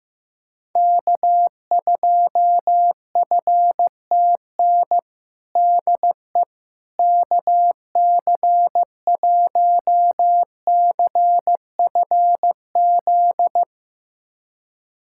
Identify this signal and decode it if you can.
Morse code, heard as a keyed tone, about 15 wpm, 700 Hz. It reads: K2FTN DE KC1CFZ
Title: USA Callsign DE Callsign 15wpm